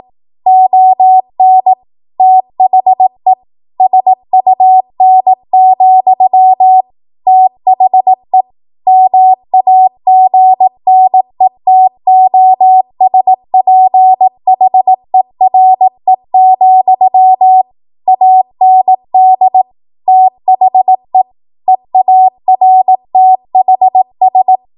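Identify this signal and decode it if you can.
Morse code, heard as a keyed tone, about 18 wpm, 750 Hz. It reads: ON THE SUN, THE MAGNETOSPHERE, AND THE EARTHS
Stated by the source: 18 WPM Morse code practice from a national amateur radio association's code archive.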